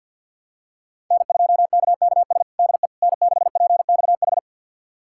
Morse code, heard as a keyed tone, about 51 wpm, 700 Hz. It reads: N2KKS DE N6PXH
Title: USA Callsign DE Callsign - 5 Characters - 50wpm